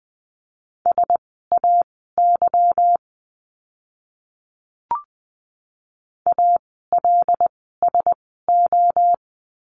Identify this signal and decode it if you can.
Morse code, heard as a keyed tone, about 20 wpm, spaced as for 14 wpm, 700 Hz. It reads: SAY ALSO
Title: 230 Lesson 37f - Letter x - Words 20wpm